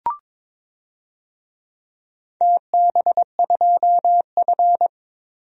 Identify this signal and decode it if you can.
Morse code, heard as a keyed tone, about 22 wpm, 690 Hz. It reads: TB2F
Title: DXCC-100 Call Signs - 4 Characters - 22wpm